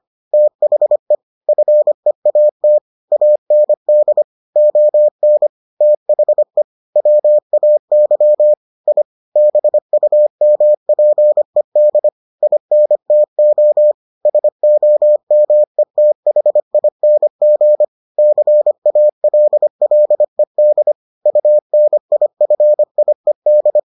Morse code, heard as a keyed tone, about 25 wpm, 600 Hz. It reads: THE FEAT AND ON THE WAY I BUMPED INTO SOMETHING CALLED UNIFIED